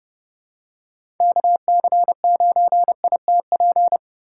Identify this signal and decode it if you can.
Morse code, heard as a keyed tone, about 30 wpm, 700 Hz. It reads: KC9ITP
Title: USA Call Signs - 6 Characters - 3x Character Spacing - 30wpm